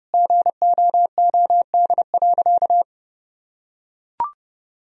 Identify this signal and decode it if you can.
Morse code, heard as a keyed tone, about 30 wpm, 700 Hz. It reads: GOOD.
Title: Sentences from Top 1000 Words - Review Part 4 - 30wpm